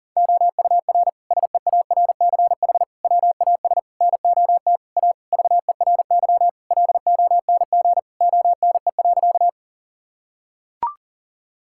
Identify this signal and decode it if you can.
Morse code, heard as a keyed tone, about 37 wpm, 700 Hz. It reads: OUR SEARCH WAS NOT A VERY LONG ONE.
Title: Sentences from Top 1000 Words - Review Part 5 - 40wpm